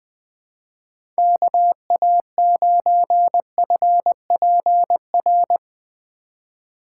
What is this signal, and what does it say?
Morse code, heard as a keyed tone, about 20 wpm, 700 Hz. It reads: KA9FPR
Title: USA Call Signs - 6 Characters - 2x Character Spacing - 20wpm